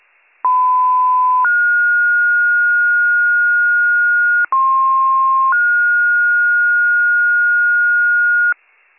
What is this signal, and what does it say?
Two-tone page, heard as a one-second tone, then a three-second tone, about 1010 then 1430 Hz, about 1060 then 1430 Hz, with noise. It is heard from 0.4 to 8.5 s.